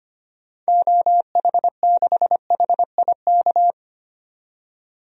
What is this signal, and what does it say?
Morse code, heard as a keyed tone, about 25 wpm, 700 Hz. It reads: OH6HIK
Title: DXCC-100 Call Signs - 6 Characters - 25wpm